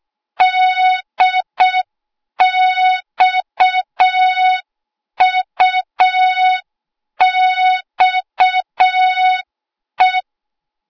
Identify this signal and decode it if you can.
Morse code, heard as a keyed tone, about 6 wpm, 3700 Hz. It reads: DXUXE
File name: Morze-HA2MN / hang / 08KXC HA2MN